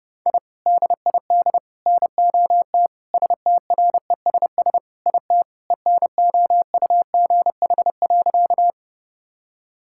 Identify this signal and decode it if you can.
Morse code, heard as a keyed tone, about 30 wpm, 700 Hz. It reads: I DID NOT STRESS IT ENOUGH.
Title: Sentences from Top 1000 Words - Review Part 6 - 30wpm